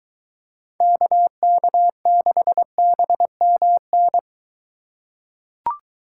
Morse code, heard as a keyed tone, about 21 wpm, 700 Hz. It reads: KK6BMN